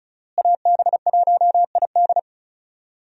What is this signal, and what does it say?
Morse code, heard as a keyed tone, about 36 wpm, 700 Hz. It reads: AB1ID
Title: USA Call Signs - 5 Characters - Repeated 3x - 35wpm